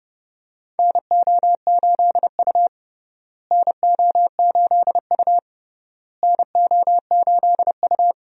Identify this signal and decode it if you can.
Morse code, heard as a keyed tone, about 30 wpm, 700 Hz. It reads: NO8U NO8U NO8U
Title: USA Call Signs - 4 Characters - Repeated 3x - 30wpm